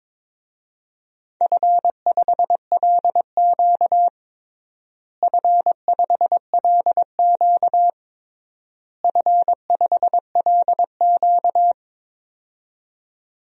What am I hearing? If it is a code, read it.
Morse code, heard as a keyed tone, about 21 wpm, 700 Hz. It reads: F5LQ F5LQ F5LQ
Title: DXCC-100 Call Signs - 4 Characters - Repeated 3x - 22wpm